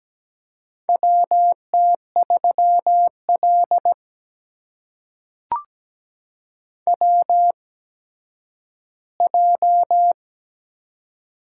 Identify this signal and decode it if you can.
Morse code, heard as a keyed tone, about 17 wpm, spaced as for 10 wpm, 700 Hz. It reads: WT3L WJ
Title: USA Call Signs - 4 Characters - 8x Character Spacing - 17wpm